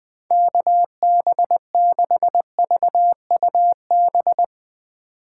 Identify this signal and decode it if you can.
Morse code, heard as a keyed tone, about 20 wpm, 700 Hz. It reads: KB6VUB
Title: Speed Racing - USA 2x3 Call Signs 17wpm